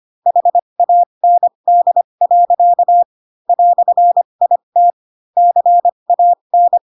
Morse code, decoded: HAND. "IT CAN

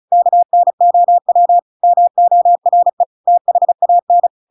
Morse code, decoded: KNOW MORE THAN

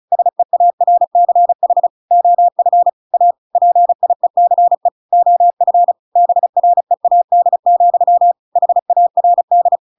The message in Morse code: SEARCH OF A PIECE OF BREAD, HARD

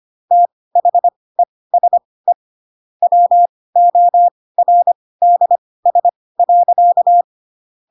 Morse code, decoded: THESE WORDS.

